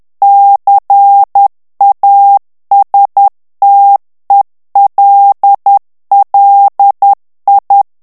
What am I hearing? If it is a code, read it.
Morse code, heard as a keyed tone, about 10 wpm, 800 Hz. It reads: CASTELLI
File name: morse mp3 / ITALIEN texte clair 1 vitesse 10 mots minute